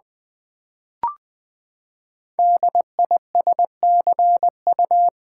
Morse code, heard as a keyed tone, about 20 wpm, 700 Hz. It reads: DISCU
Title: Build A Word - 5 Letters At A Time - 10-Letter Words - 20wpm